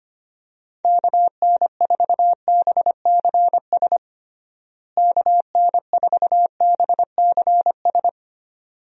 Morse code, heard as a keyed tone, about 25 wpm, 700 Hz. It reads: KN4BCS KN4BCS
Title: USA Call Signs - 6 Characters - Repeated 2x - 25wpm